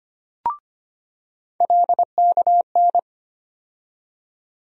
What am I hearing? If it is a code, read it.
Morse code, heard as a keyed tone, about 25 wpm, 700 Hz. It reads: LKN
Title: USA Call Sign Postfixes - 3 Letters 25wpm